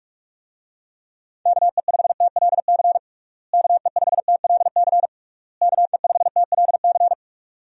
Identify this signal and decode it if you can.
Morse code, heard as a keyed tone, about 42 wpm, 700 Hz. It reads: KE5TLC KE5TLC KE5TLC
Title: USA Call Signs - 6 Characters - Repeated 3x - 45wpm